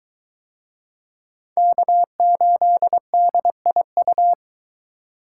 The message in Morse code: K8DIU

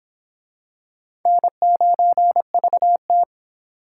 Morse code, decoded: N9VT

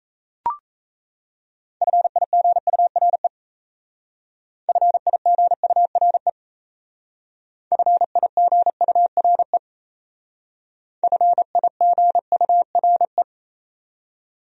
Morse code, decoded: FIGURE FIGURE FIGURE FIGURE

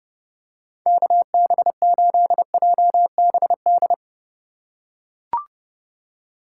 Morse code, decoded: KB8JBD

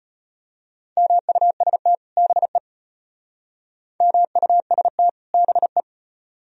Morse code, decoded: MUST BE MUST BE

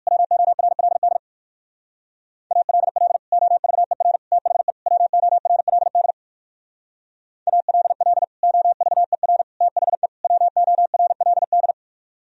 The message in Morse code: WORLD ALL OVER THE WORLD ALL OVER THE WORLD

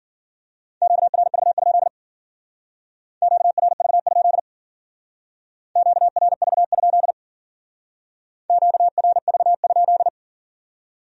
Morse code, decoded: QRV? QRV? QRV? QRV?

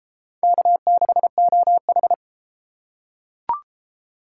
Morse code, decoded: K6OH

